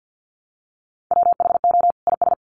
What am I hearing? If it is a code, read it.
Morse code, heard as a keyed tone, about 50 wpm, 700 Hz. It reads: WHO IS
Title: Sets of 2 Words 50wpm